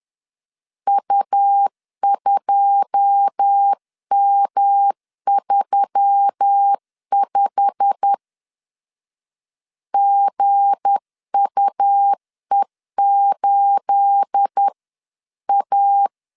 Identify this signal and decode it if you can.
Morse code, heard as a keyed tone, about 10 wpm, 800 Hz. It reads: U2M35 GUE8A
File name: LEZIONI / 9 28 10wpm 2842